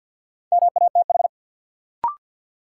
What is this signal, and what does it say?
Morse code, heard as a keyed tone, about 53 wpm, 700 Hz. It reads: MATH
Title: Sentences from Top 1000 Words - Review Part 6 - New Word Drill - 50wpm